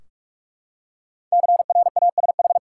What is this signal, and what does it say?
Morse code, heard as a keyed tone, about 46 wpm, 700 Hz. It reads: CRASH
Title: Five Letter Words - 45wpm